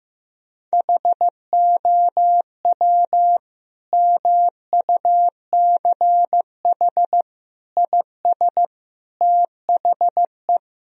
Morse code, heard as a keyed tone, about 15 wpm, 700 Hz. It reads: HOW MUCH IS THE
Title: Sentences from Top 1000 Words - Review Part 3 - 15wpm